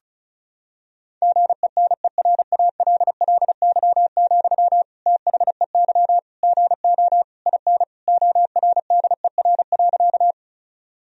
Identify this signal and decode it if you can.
Morse code, heard as a keyed tone, about 34 wpm, 700 Hz. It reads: GENERALLY, THEY GO IN ORDER.
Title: Sentences from Top 1000 Words - Review Part 4 - 35wpm